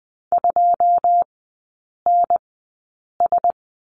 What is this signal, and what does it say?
Morse code, heard as a keyed tone, about 20 wpm, spaced as for 8 wpm, 700 Hz. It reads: 2NS
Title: 121 Lesson 20e - Number 3 - Three Characters 20wpm